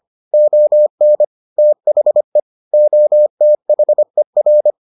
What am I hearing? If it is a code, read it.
Morse code, heard as a keyed tone, about 25 wpm, 600 Hz. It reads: ON THE OTHER